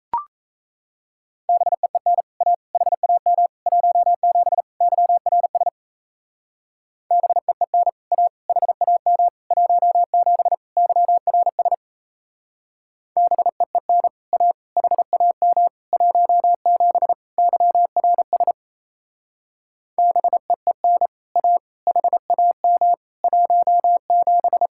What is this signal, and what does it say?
Morse code, heard as a keyed tone, about 34 wpm, 700 Hz. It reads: BEEN A HAM 17 YRS BEEN A HAM 17 YRS BEEN A HAM 17 YRS BEEN A HAM 17
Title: Speed-Racing Common QSO Phrases - 28wpm